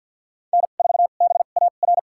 Morse code, heard as a keyed tone, about 50 wpm, spaced as for 36 wpm, 690 Hz. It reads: N4BAR
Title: USA Call Signs - 5 Characters - 2x Character Spacing - 50wpm